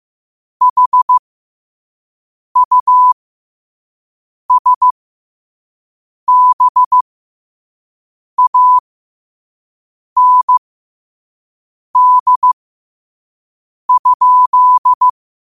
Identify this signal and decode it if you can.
Morse code, heard as a keyed tone, about 15 wpm, spaced as for 5 wpm, 1000 Hz. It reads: HUSBAND?
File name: qotdfiles / 5wpm